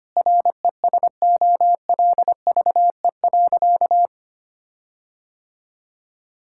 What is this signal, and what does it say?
Morse code, heard as a keyed tone, about 25 wpm, 700 Hz. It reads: RESOLVE.
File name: practice / 25wpm / Difficult Sentences with Hints - 25wpm